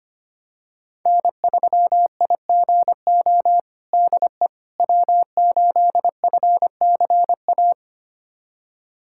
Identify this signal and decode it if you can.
Morse code, heard as a keyed tone, about 25 wpm, 700 Hz. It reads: N3IGO DE W8FCA
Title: USA Callsign DE Callsign - 5 Characters - 25wpm